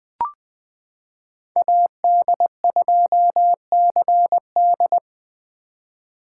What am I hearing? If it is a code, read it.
Morse code, heard as a keyed tone, about 20 wpm, 700 Hz. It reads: AD2CD
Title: 2023 POTA Hunter Call Signs - 20wpm